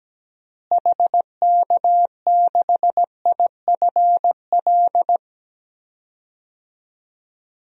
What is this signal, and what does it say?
Morse code, heard as a keyed tone, about 17 wpm, 700 Hz. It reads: HK6IFL
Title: DXCC-100 Call Signs - 6 Characters - Repeated 2x - 17wpm